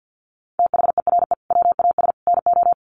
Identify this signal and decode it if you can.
Morse code, heard as a keyed tone, about 50 wpm, 700 Hz. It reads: THERE WAS NO
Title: Sets of 3 Words 50wpm